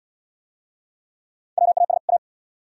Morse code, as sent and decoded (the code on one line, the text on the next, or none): .-- .- ... / .-
WAS A